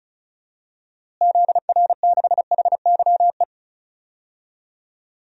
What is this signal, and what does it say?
Morse code, heard as a keyed tone, about 36 wpm, 700 Hz. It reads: ZR6HYE
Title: DXCC-100 Call Signs - 6 Characters - Repeated 2x - 35wpm